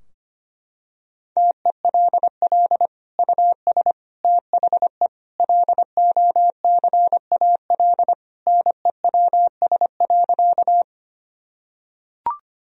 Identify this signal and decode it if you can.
Morse code, heard as a keyed tone, about 25 wpm, 700 Hz. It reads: TELL US THE LOCAL NEWS.
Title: Sentences from Top 1000 Words - Review Part 3 - 25wpm